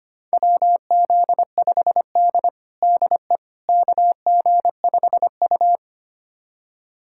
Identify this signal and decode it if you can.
Morse code, heard as a keyed tone, about 25 wpm, 700 Hz. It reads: WZ5D DE KG5U
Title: USA Callsign DE Callsign - 4 Characters - 25wpm